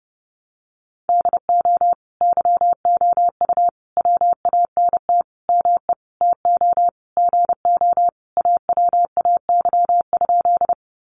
Morse code, heard as a keyed tone, about 30 wpm, 700 Hz. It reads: DO YOU WANT ME TO GO AWAY?
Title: Sentences from Top 600 Words 30wpm